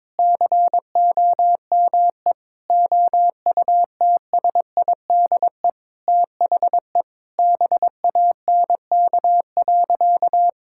Morse code, heard as a keyed tone, about 22 wpm, 700 Hz. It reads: COME OUTSIDE THE BANK.
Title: Sentences from Top 1000 Words - Review Part 6 - 22wpm